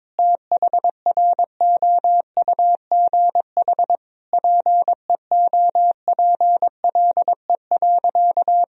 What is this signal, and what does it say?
Morse code, heard as a keyed tone, about 22 wpm, 700 Hz. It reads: THROUGH PEOPLE.
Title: Sentences from Top 200 Words and Abbreviations 22wpm